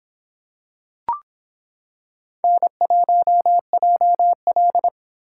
Morse code, heard as a keyed tone, about 26 wpm, 700 Hz. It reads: N1JL